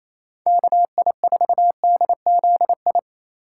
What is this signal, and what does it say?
Morse code, heard as a keyed tone, about 28 wpm, 690 Hz. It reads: KI4DZI
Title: USA Call Signs - 6 Characters - 2x Character Spacing - 28wpm